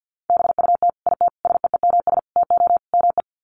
Morse code, heard as a keyed tone, about 50 wpm, 700 Hz. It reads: BUT IT SEEMS TO ME
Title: Sets of 5 Words Encore2 50wpm